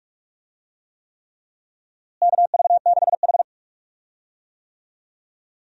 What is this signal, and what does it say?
Morse code, heard as a keyed tone, about 46 wpm, 700 Hz. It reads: KV6H
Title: USA Call Signs - 4 Characters - Repeated 3x - 45wpm